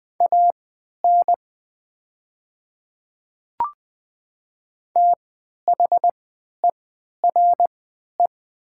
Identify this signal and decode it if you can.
Morse code, heard as a keyed tone, about 20 wpm, spaced as for 11 wpm, 700 Hz. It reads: AN THERE
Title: 3x Character Spacing - Top 100 Words - 20wpm